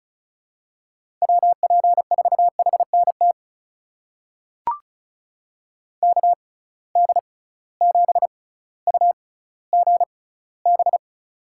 Morse code, heard as a keyed tone, about 36 wpm, spaced as for 18 wpm, 700 Hz. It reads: WP4HNT KD7UGB
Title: USA Call Signs - 6 Characters - 6x Character Spacing - 35wpm